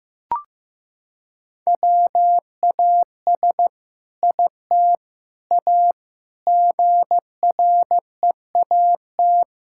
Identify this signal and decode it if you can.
Morse code, heard as a keyed tone, about 15 wpm, 700 Hz. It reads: WAS IT A GREAT
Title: Sentences from Top 1000 Words - Review Part 5 - 15wpm